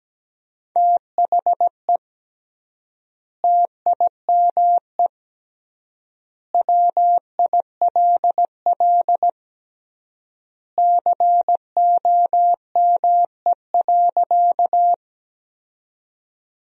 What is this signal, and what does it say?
Morse code, heard as a keyed tone, about 17 wpm, 700 Hz. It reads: THE TIME WILL COME.
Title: 3x Word-Spacing - Sentences from Top 100 Words - 17wpm